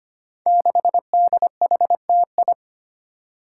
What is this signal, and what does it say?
Morse code, heard as a keyed tone, about 25 wpm, 700 Hz. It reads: 6DHTI
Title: Five Challenging Letter-Number Combinations 25wpm